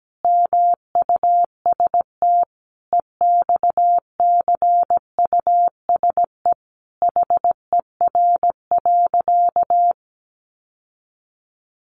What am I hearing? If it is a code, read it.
Morse code, heard as a keyed tone, about 17 wpm, 700 Hz. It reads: MUST EXCUSE HER.
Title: Sentences from Top 700 Words 17wpm